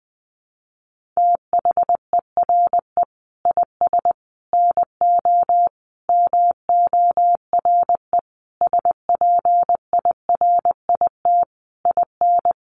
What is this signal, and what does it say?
Morse code, heard as a keyed tone, about 20 wpm, 700 Hz. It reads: THERE IS NO MORE SPIRIT IN
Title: Sentences from Top 1400 Words 20wpm